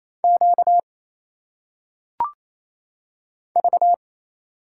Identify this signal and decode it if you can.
Morse code, heard as a keyed tone, about 27 wpm, 700 Hz. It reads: Q V
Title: Single Letters - Phonetic Pronunciation 28wpm